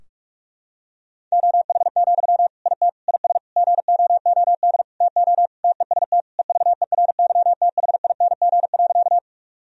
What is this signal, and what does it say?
Morse code, heard as a keyed tone, about 45 wpm, 700 Hz. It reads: OH, IT IS GOOD TO TEST EVERYTHING.
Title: Sentences from Top 1000 Words - Review Part 3 - 45wpm